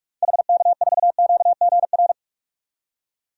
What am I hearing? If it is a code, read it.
Morse code, heard as a keyed tone, about 47 wpm, 700 Hz. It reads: HK4QGR
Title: DXCC-100 Call Signs - 6 Characters - 45wpm